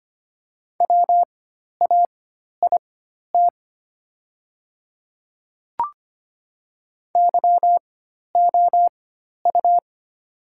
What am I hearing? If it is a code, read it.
Morse code, heard as a keyed tone, about 25 wpm, spaced as for 12 wpm, 700 Hz. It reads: WAIT YOU
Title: 4x Character Spacing - Top 500 Words - 25wpm